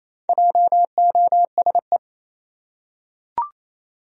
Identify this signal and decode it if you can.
Morse code, heard as a keyed tone, about 27 wpm, 700 Hz. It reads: JOSE
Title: USA Common Names - Max 4 Letters - 28wpm